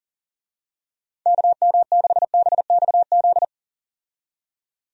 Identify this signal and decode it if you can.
Morse code, heard as a keyed tone, about 39 wpm, 700 Hz. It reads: KM6BXZ
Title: USA Call Signs - 6 Characters - 40wpm